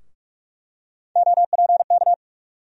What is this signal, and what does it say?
Morse code, heard as a keyed tone, about 42 wpm, 700 Hz. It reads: OPK
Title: USA Call Sign Postfixes - 3 Letters 45wpm